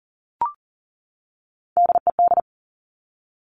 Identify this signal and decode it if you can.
Morse code, heard as a keyed tone, about 39 wpm, 700 Hz. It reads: DED
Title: Top Three Letter Combinations in Words 40wpm